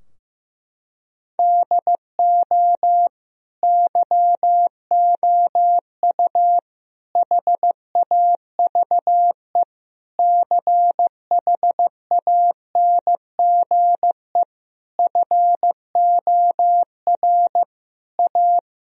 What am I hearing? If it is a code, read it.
Morse code, heard as a keyed tone, about 15 wpm, 700 Hz. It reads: DO YOU HAVE CHANGE FOR A
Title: Sentences from Top 1000 Words - Review Part 2 - 15wpm